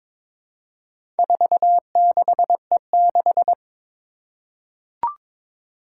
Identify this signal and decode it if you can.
Morse code, heard as a keyed tone, about 22 wpm, 700 Hz. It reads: 46E6
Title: Four Challenging Letter-Number Combinations 22wpm